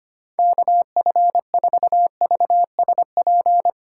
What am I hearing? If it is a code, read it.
Morse code, heard as a keyed tone, about 31 wpm, 700 Hz. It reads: KF4VSP
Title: Speed Racing - USA 2x3 Call Signs 25wpm